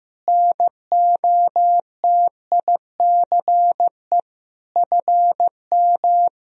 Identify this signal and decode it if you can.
Morse code, heard as a keyed tone, about 15 wpm, 700 Hz. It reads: NOTICE FM
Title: Sentences from Top 400 Words and Abbreviations 15wpm